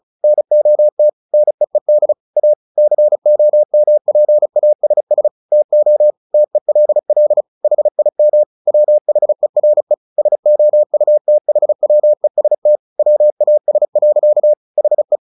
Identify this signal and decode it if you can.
Morse code, heard as a keyed tone, about 34 wpm, 600 Hz. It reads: NOT NEED A COMPASS TO TELL HIM WHERE SOUTHWEST WAS. HE
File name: the old man and the sea 2 35wpm